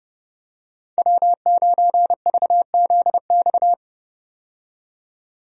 Morse code, heard as a keyed tone, about 30 wpm, 700 Hz. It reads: W9VZX
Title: USA Call Signs - 5 Characters - 30wpm